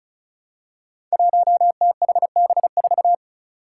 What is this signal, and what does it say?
Morse code, heard as a keyed tone, about 34 wpm, 700 Hz. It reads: 1THB4
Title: Five Challenging Letter-Number Combinations 35wpm